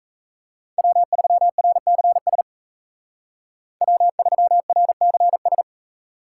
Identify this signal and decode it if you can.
Morse code, heard as a keyed tone, about 34 wpm, 700 Hz. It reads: W3RCS W3RCS
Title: Speed Racing - USA 1x3 Call Signs 28wpm